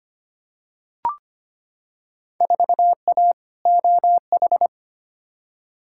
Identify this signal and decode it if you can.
Morse code, heard as a keyed tone, about 25 wpm, 700 Hz. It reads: 4A OH
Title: Field Day Exchanges - 25wpm